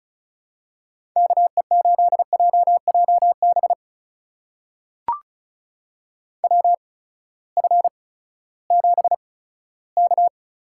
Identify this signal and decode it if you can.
Morse code, heard as a keyed tone, about 36 wpm, spaced as for 18 wpm, 700 Hz. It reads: KE8JJB WF7K
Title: USA Call Signs - 8x Character Spacing - 35wpm